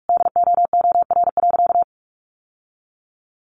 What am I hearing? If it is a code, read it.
Morse code, heard as a keyed tone, about 44 wpm, 700 Hz. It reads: DOOR.
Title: Sentences from Top 1300 Words 45wpm